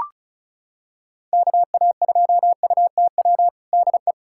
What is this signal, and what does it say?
Morse code, heard as a keyed tone, about 36 wpm, 700 Hz. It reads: KA2UTW DE
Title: USA Callsign DE Callsign 35wpm